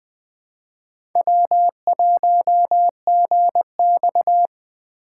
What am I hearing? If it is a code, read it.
Morse code, heard as a keyed tone, about 20 wpm, 700 Hz. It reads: W1GX
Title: USA Call Signs - 4 Characters - 3x Character Spacing - 20wpm